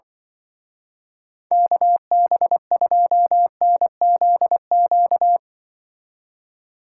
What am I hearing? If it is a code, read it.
Morse code, heard as a keyed tone, about 24 wpm, 700 Hz. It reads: KB2NZQ
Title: Speed Racing - USA 2x3 Call Signs 20wpm